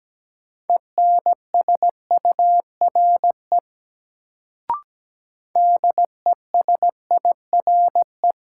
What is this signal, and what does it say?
Morse code, heard as a keyed tone, about 17 wpm, 700 Hz. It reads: ENSURE DESIRE
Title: Sentences from Top 1100 Words - New Word Drill - 17wpm